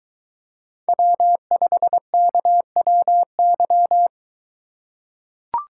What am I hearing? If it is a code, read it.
Morse code, heard as a keyed tone, about 23 wpm, 700 Hz. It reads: W5KWY